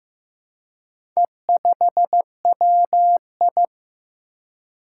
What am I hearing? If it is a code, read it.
Morse code, heard as a keyed tone, about 15 wpm, 700 Hz. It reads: E5WI
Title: Four Challenging Letter-Number Combinations 15wpm